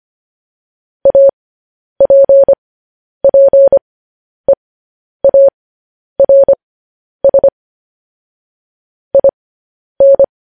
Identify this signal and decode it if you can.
Morse code, heard as a keyed tone, about 25 wpm, spaced as for 10 wpm, 550 Hz. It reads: APPEARS IN
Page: news-headlines-in-morse-code-at-10-wpm